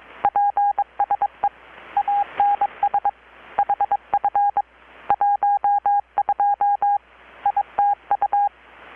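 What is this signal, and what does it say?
Morse code, heard as a keyed tone, about 22 wpm, 810 Hz. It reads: PSE ANS HF 12 ITU
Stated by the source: Morse from KPH's Night of Nights XX maritime radiotelegraphy event, received in Chiapas, Mexico about 3,800 km away on a Kenwood R-600 with a grounded 10 m long-wire antenna.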